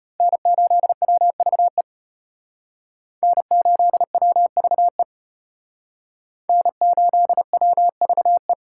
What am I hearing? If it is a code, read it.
Morse code, heard as a keyed tone, about 32 wpm, 700 Hz. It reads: N8WVE N8WVE N8WVE